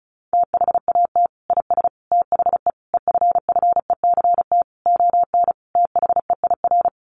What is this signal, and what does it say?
Morse code, heard as a keyed tone, about 34 wpm, 700 Hz. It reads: THAT IS THE EFFECT ON THEIR
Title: Sentences from Top 1300 Words 35wpm